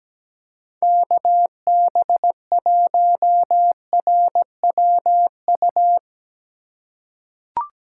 Morse code, heard as a keyed tone, about 17 wpm, 700 Hz. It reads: KB1RWU
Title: USA Call Signs - 4x Character Spacing - 17wpm